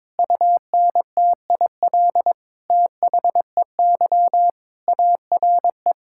Morse code, heard as a keyed tone, about 22 wpm, 700 Hz. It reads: UNTIL THEY ARE
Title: Sentences from Top 1600 Words 22wpm